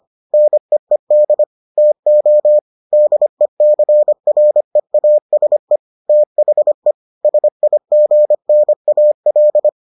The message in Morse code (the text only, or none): NEED TO DECREASE THE SIGNAL